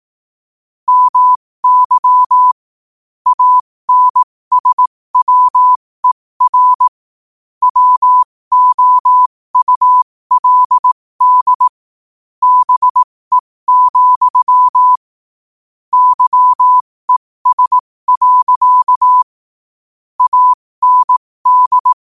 Morse code, heard as a keyed tone, about 18 wpm, spaced as for 15 wpm, 1000 Hz. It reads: MY ANSWER WOULD BE, YES. AND
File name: qotdfiles / 15wpm